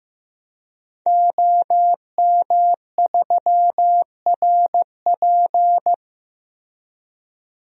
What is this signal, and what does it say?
Morse code, heard as a keyed tone, about 15 wpm, 700 Hz. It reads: OM3RP
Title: DXCC-100 Call Signs - 5 Characters - Repeated 3x - 15wpm